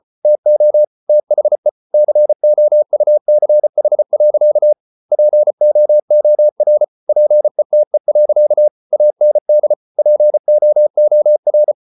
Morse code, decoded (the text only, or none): TO THE COUCH. POOR PETE. AND POOR